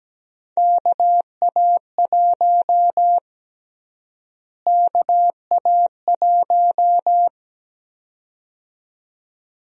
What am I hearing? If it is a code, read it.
Morse code, heard as a keyed tone, about 17 wpm, 700 Hz. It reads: KA1 KA1
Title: USA Call Sign Prefixes - Repeated 2x - 17wpm